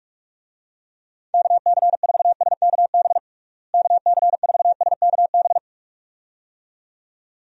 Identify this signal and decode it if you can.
Morse code, heard as a keyed tone, about 44 wpm, 700 Hz. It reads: KC4SKB KC4SKB
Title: USA Call Signs - 6 Characters - Repeated 2x - 45wpm